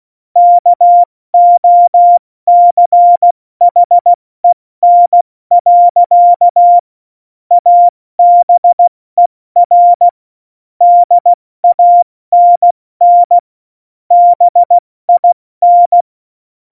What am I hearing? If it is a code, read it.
Morse code, heard as a keyed tone, about 16 wpm, 700 Hz. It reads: KOCHEN. ABER DANN BIN
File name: Morsekurs / Witze / Witz 20. August 2019, 70 F80